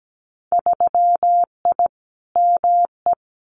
Morse code, heard as a keyed tone, about 17 wpm, 700 Hz. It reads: 3I ME
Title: Winter Field Day Exchanges 17wpm